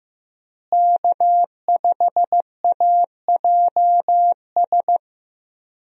Morse code, heard as a keyed tone, about 15 wpm, 700 Hz. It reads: K5AJS
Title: USA Call Signs - 5 Characters - Repeated 3x - 15wpm